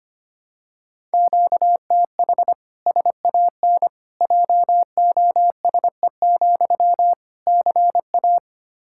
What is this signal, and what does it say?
Morse code, heard as a keyed tone, about 25 wpm, 700 Hz. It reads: QTH SAN JOSE, CA
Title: Practice QSO - Exchange 1 - Beginning - 25wpm